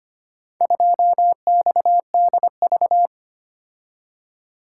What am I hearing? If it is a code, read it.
Morse code, heard as a keyed tone, about 25 wpm, 700 Hz. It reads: 2XDV
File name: practice / 25wpm / Four Letter-Number Combinations - 25wpm